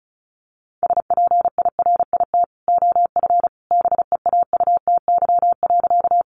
Morse code, heard as a keyed tone, about 34 wpm, 700 Hz. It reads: SPIRIT OF BEAUTY.
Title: Sentences from Top 1400 Words 35wpm